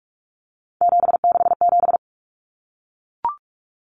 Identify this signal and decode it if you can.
Morse code, heard as a keyed tone, about 44 wpm, 700 Hz. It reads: 767
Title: Numbers - 3-Digit 45wpm